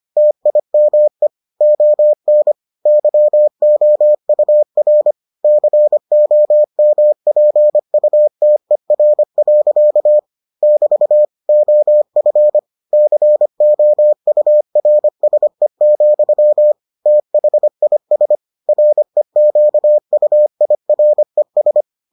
Morse code, heard as a keyed tone, about 25 wpm, 600 Hz. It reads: TIME ON YOUR COMPUTER. <BT> OF COURSE, THIS REQUIRES